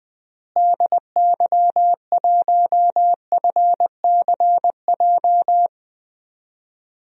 Morse code, heard as a keyed tone, about 20 wpm, 700 Hz. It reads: DY1FCJ
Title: DXCC-100 Call Signs - 6 Characters - Repeated 3x - 20wpm